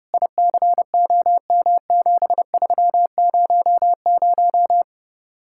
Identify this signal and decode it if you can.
Morse code, heard as a keyed tone, about 38 wpm, 700 Hz. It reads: ICOM7300